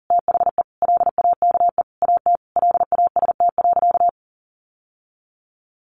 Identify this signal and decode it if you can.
Morse code, heard as a keyed tone, about 40 wpm, 700 Hz. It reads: THE LAKE AT LAST.
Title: Sentences from Top 1100 Words 40wpm